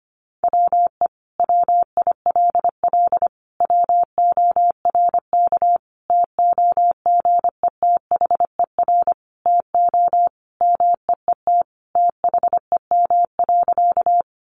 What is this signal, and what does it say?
Morse code, heard as a keyed tone, about 25 wpm, 700 Hz. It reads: WE WILL WORK TOGETHER TO MEET THEM.
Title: Sentences from Top 1100 Words 25wpm